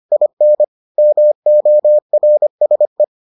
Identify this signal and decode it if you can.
Morse code, heard as a keyed tone, about 25 wpm, 600 Hz. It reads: IN MORSE